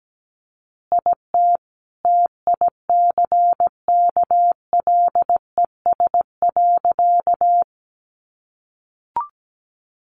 Morse code, heard as a keyed tone, about 17 wpm, 700 Hz. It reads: IT TICKLES.
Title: Additional Easy Phrases 17wpm